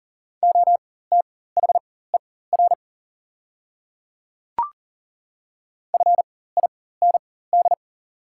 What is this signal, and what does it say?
Morse code, heard as a keyed tone, about 40 wpm, spaced as for 19 wpm, 700 Hz. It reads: OTHER FIND
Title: 4x Character Spacing - Top 100 Words - 40wpm